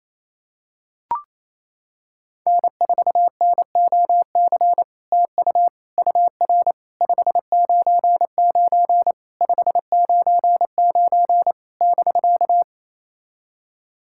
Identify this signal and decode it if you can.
Morse code, heard as a keyed tone, about 28 wpm, 700 Hz. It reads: N4NOC TU UR 599 599 <BK>
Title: SOTA QSO Practice 28wpm